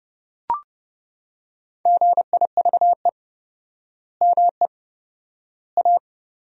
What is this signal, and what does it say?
Morse code, heard as a keyed tone, about 30 wpm, 700 Hz. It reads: GIVE ME A
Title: 4x Word-Spacing - Sentences from Top 200 Words - 30wpm